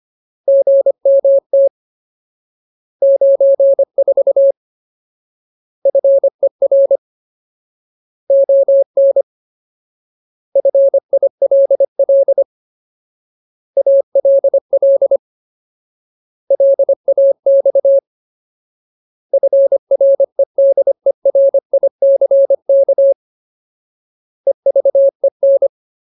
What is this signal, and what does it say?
Morse code, heard as a keyed tone, about 25 wpm, 550 Hz. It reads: GMT 94 FER ON FILL ALL LAX FREDERICK EVEN
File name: Ultimate Word List / 25WPM Ultimate